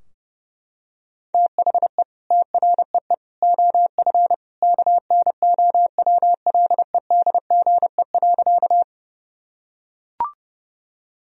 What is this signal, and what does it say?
Morse code, heard as a keyed tone, about 30 wpm, 700 Hz. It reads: THE TREE OF KNOWLEDGE.
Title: Sentences from Top 1000 Words - Review Part 3 - 30wpm